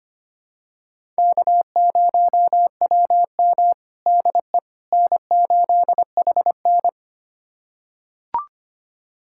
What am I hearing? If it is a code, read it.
Morse code, heard as a keyed tone, about 25 wpm, 700 Hz. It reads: K0WM DE N8HN
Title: USA Callsign DE Callsign - 4 Characters - 25wpm